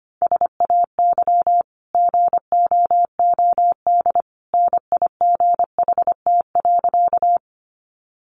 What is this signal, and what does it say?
Morse code, heard as a keyed tone, about 25 wpm, 700 Hz. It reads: SAY GOOD NIGHT.
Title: Sentences from Top 300 Words 25wpm